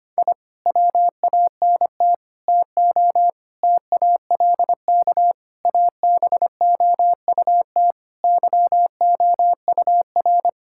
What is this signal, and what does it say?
Morse code, heard as a keyed tone, about 25 wpm, 700 Hz. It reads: I WANT TO TALK ABOUT YOUR
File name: practice / 25wpm / Sentences with 4-Word Phrases - 25wpm